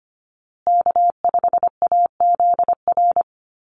Morse code, heard as a keyed tone, about 25 wpm, 700 Hz. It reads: K5AZR
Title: Call Signs Repeated 2x - 25wpm